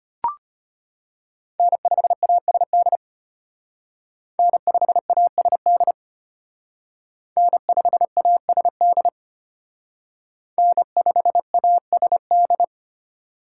Morse code, heard as a keyed tone, about 32 wpm, 700 Hz. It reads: N5ASD N5ASD N5ASD N5ASD